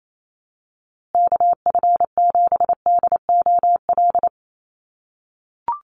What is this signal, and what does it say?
Morse code, heard as a keyed tone, about 28 wpm, 700 Hz. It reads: KF7DOL
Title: USA Call Signs Repeated 2x 28wpm